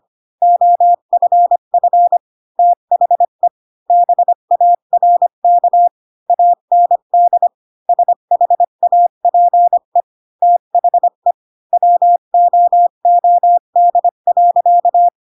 Morse code, heard as a keyed tone, about 25 wpm, 700 Hz. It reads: OFF THE BARK AND SHAPE THE WOOD.
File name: Pinocchio / Pinocchio ch1 20 25 ik1hge